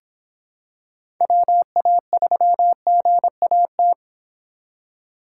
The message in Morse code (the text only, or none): WA3GAT